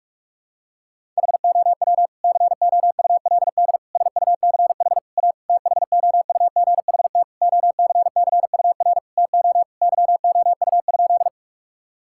Morse code, heard as a keyed tone, about 48 wpm, 700 Hz. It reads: HOW COULD SUCH A THOUGHT OCCUR TO YOU?
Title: Sentences from Top 1000 Words - Review Part 6 - 45wpm